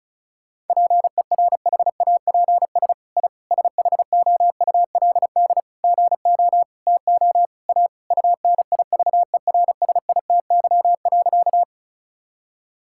Morse code, heard as a keyed tone, about 36 wpm, 700 Hz. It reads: PERHAPS I SHOULD GO TO A UNIVERSITY.
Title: Sentences from Top 1000 Words - Review Part 5 - 35wpm